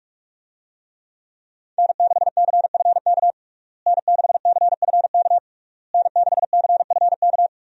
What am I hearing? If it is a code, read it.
Morse code, heard as a keyed tone, about 44 wpm, 700 Hz. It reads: N6CFK N6CFK N6CFK
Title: USA Call Signs - 5 Characters - Repeated 3x - 45wpm